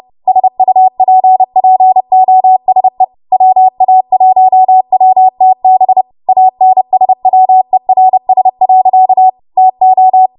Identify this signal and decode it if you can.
Morse code, heard as a keyed tone, about 30 wpm, 750 Hz. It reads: SUPPOSE WA1WTB ANSWERS. TO
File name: Morse / 30WPM